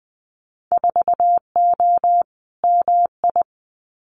Morse code, heard as a keyed tone, about 20 wpm, 700 Hz. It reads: 4O MI
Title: Winter Field Day Exchanges Encore 20wpm